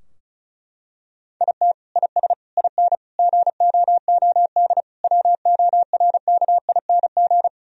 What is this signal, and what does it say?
Morse code, heard as a keyed tone, about 36 wpm, 700 Hz. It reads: IT IS IN GOOD WORKING
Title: Sentences from Top 1000 Words - Review Part 4 - 35wpm